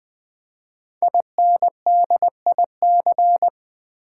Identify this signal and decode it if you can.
Morse code, heard as a keyed tone, about 20 wpm, 700 Hz. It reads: INDIC
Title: Build A Word - 5 Letters At A Time - 10-Letter Words - 20wpm